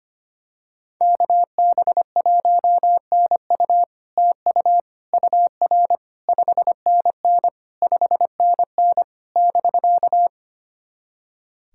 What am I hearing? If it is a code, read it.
Morse code, heard as a keyed tone, about 25 wpm, 700 Hz. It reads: KB1NU TU UR 5NN 5NN <BK>
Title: SOTA QSO Practice 25wpm